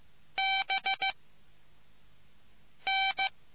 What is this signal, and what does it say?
Morse code, heard as a keyed tone, about 15 wpm, 770 Hz. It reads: B N